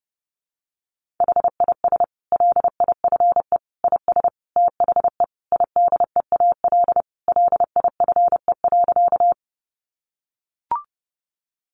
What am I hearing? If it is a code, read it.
Morse code, heard as a keyed tone, about 30 wpm, 700 Hz. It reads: HIS LIFE IS THE IDEAL LIFE.
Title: Sentences from Top 1400 Words 30wpm